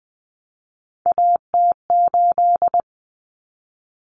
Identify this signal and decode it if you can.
Morse code, heard as a keyed tone, about 20 wpm, 700 Hz. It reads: AT8
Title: Common Cut Numbers - 3-Digit 20wpm